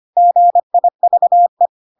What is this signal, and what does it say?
Morse code, heard as a keyed tone, about 25 wpm, 700 Hz. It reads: GIVE
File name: Pinocchio / Pinocchio ch3 25 25 ik1hge